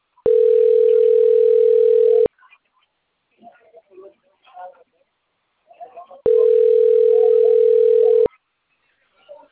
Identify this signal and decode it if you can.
Ringback tone (North America), 440 + 480 Hz.
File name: rg-1600-unknown- - - .78362